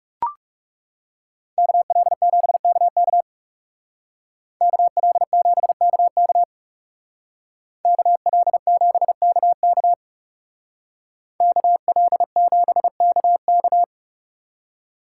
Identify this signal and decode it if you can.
Morse code, heard as a keyed tone, about 35 wpm, 700 Hz. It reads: KL7KK KL7KK KL7KK KL7KK
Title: Speed Racing - USA Call Signs 30wpm